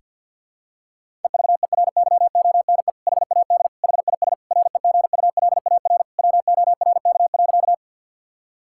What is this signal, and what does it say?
Morse code, heard as a keyed tone, about 52 wpm, 700 Hz. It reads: EVERYONE HAD HIS REGULAR WORK.
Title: Sentences from Top 1000 Words - Review Part 6 - 50wpm